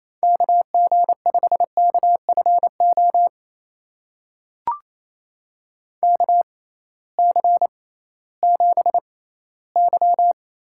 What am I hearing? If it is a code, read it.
Morse code, heard as a keyed tone, about 28 wpm, spaced as for 16 wpm, 700 Hz. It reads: KG5KFO KC7Y
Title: USA Call Signs - 6 Characters - 6x Character Spacing - 28wpm